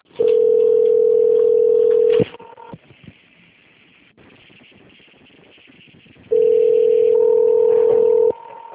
Ringback tone (North America), 440 + 480 Hz.
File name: out- - - - - .1948